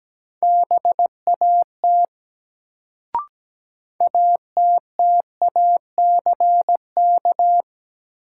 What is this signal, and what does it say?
Morse code, heard as a keyed tone, about 17 wpm, 700 Hz. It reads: BAT ATTACK
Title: Sentences from Top 1500 Words - New Word Drill Alt - 17wpm